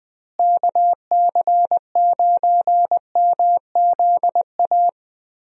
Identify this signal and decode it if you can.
Morse code, heard as a keyed tone, about 20 wpm, 700 Hz. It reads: KC9MZA
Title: USA Call Signs - 6 Characters - Repeated 3x - 20wpm